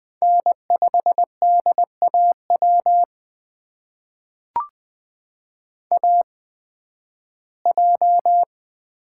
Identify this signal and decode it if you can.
Morse code, heard as a keyed tone, about 20 wpm, spaced as for 13 wpm, 700 Hz. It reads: N5DAW AJ
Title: USA Call Signs - 5 Characters - 8x Character Spacing - 20wpm